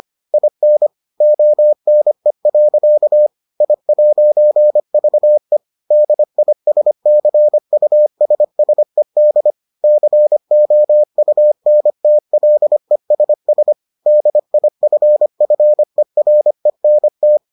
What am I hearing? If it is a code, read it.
Morse code, heard as a keyed tone, about 25 wpm, 600 Hz. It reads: IN ONE. I'VE DISCUSSED COUNTLESS DIFFERENT